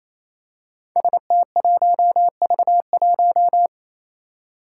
Morse code, heard as a keyed tone, about 28 wpm, 700 Hz. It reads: ST1V1
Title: Five Challenging Letter-Number Combinations 28wpm